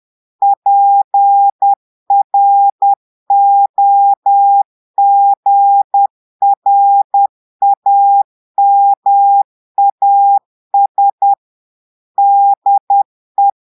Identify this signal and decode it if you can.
Morse code, heard as a keyed tone, about 10 wpm, 800 Hz. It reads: PROGRAMAS DE
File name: portugais texte en clair 2 vitesse 10 mots minutes